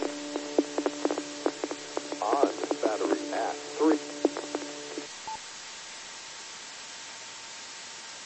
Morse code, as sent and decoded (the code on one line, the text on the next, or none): none